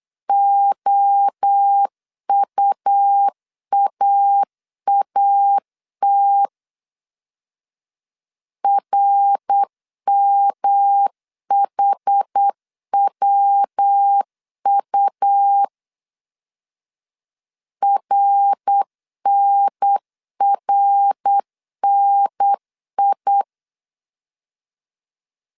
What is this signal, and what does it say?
Morse code, heard as a keyed tone, about 8 wpm, 800 Hz. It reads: OUAAT RMHWU RNRNI
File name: CW / DET / 1 3 1500C S B 8wpm 4141